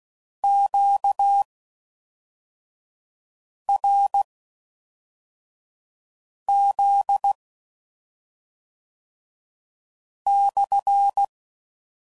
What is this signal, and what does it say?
Morse code, heard as a keyed tone, about 16 wpm, spaced as for 4 wpm, 800 Hz. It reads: QRZ /